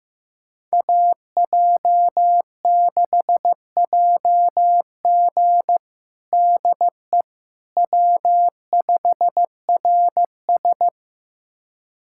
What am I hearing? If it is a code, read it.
Morse code, heard as a keyed tone, about 15 wpm, 700 Hz. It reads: AJ6JG DE W5RS
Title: USA Callsign DE Callsign 15wpm